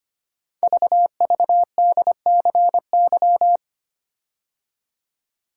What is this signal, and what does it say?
Morse code, heard as a keyed tone, about 25 wpm, 700 Hz. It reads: VVDCY
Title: Code Groups - 5 Letters - 25wpm